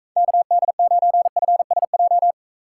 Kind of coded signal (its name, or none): Morse code